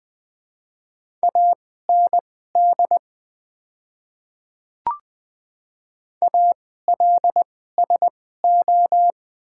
Morse code, heard as a keyed tone, about 21 wpm, spaced as for 15 wpm, 700 Hz. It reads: AND ALSO
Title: 2x Character Spacing - Top 200 Words - 20wpm